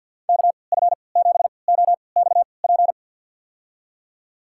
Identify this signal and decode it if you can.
Morse code, heard as a keyed tone, about 51 wpm, spaced as for 29 wpm, 700 Hz. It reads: KF7OXP
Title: USA Call Signs - 3x Character Spacing - 50wpm